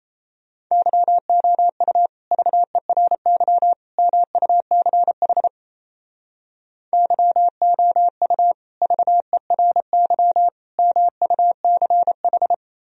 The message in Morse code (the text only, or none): YOU VERY MUCH YOU VERY MUCH